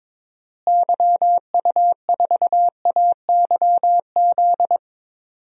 Morse code, heard as a keyed tone, about 22 wpm, 700 Hz. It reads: YU4AYZ
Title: DXCC-100 Call Signs - 6 Characters - Repeated 3x - 22wpm